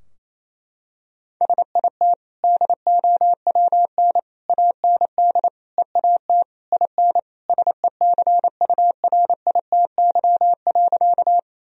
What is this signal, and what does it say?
Morse code, heard as a keyed tone, about 28 wpm, 700 Hz. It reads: SIT DOWN AND EAT IN SECURITY.
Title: Sentences from Top 1000 Words - Review Part 4 - 28wpm